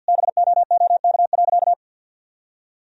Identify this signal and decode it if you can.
Morse code, heard as a keyed tone, about 48 wpm, 700 Hz. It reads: BOOK.
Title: Sentences from Top 1000 Words - Review Part 1 - 50wpm